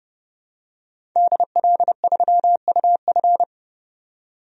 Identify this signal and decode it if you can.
Morse code, heard as a keyed tone, about 30 wpm, 700 Hz. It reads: DL3UF
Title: DXCC-100 Call Signs - 5 Characters - 30wpm